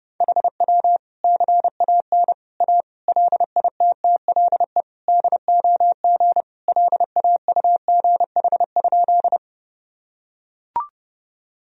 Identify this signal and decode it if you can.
Morse code, heard as a keyed tone, about 30 wpm, 700 Hz. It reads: HW CAN A LITTLE DOG LAUGH?
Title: Sentences from Top 500 Words and Abbreviations 30wpm